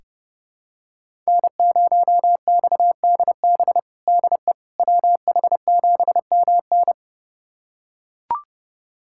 Morse code, heard as a keyed tone, about 30 wpm, 700 Hz. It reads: N0XDB DE WH7MN
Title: USA Callsign DE Callsign - 5 Characters - 30wpm